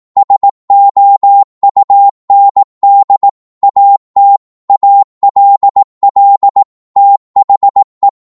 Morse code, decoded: SOUND AT ALL THE